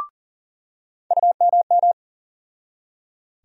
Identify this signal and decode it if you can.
Morse code, heard as a keyed tone, about 40 wpm, 700 Hz. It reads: UMM
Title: USA Call Sign Postfixes - 3 Letters 40wpm